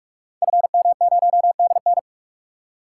Morse code, heard as a keyed tone, about 44 wpm, 700 Hz. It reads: FM0DN
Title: Code Groups - 5 Letter:Numbers - 45wpm